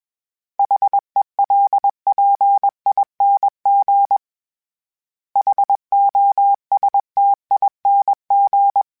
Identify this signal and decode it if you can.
Morse code, heard as a keyed tone, about 21 wpm, 800 Hz. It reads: HELPING HOSTING
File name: Special INGs / Special INGs-mp3s / Sngl Ing @20